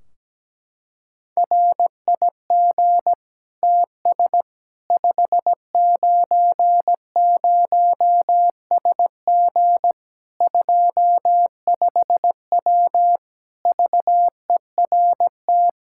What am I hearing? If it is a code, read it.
Morse code, heard as a keyed tone, about 17 wpm, 700 Hz. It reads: RIG TS 590SG 25W VERT
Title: Practice QSO - Exchange 2 - Info - 17wpm